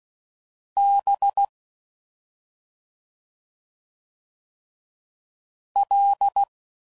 Morse code, heard as a keyed tone, about 16 wpm, 800 Hz. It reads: B L
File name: Suffixes / Special ING's / Sngl ING's